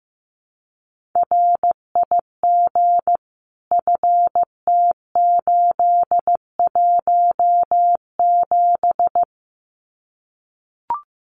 Morse code, heard as a keyed tone, about 15 wpm, 700 Hz. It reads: RIG FT817
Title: Common QSO Phrases Encore 15wpm